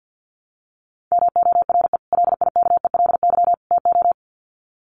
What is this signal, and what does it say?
Morse code, heard as a keyed tone, about 50 wpm, 700 Hz. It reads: MORE LIKELY TO
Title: Sets of 3 Words 50wpm